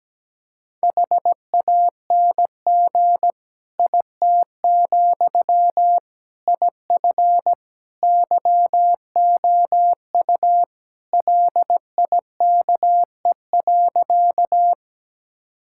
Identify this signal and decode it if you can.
Morse code, heard as a keyed tone, about 17 wpm, 700 Hz. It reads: HANG IT, IF YOU LIKE.
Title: Sentences from Top 1600 Words 17wpm